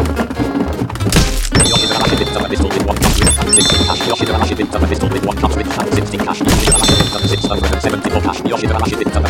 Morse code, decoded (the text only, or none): none